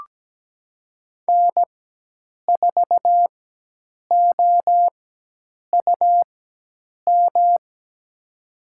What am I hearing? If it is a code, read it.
Morse code, heard as a keyed tone, about 17 wpm, spaced as for 8 wpm, 700 Hz. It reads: N4OUM
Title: USA Call Signs - 5 Characters - 4x Character Spacing - 17wpm